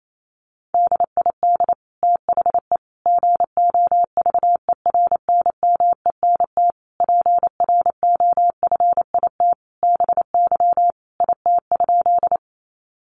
Morse code, heard as a keyed tone, about 28 wpm, 700 Hz. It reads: DID THE GOVERNMENT PROFIT BY IT?
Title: Sentences from Top 800 Words 28wpm